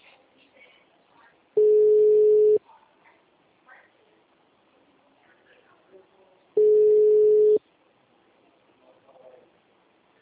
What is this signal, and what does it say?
Ringback tone (Europe, 425 Hz), with speech.
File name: out- -2012- - - .10